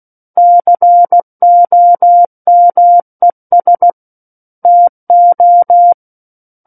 Morse code, heard as a keyed tone, about 16 wpm, 700 Hz. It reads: COMES TO
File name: Book One / 07-Chapter 7 - 16 WPM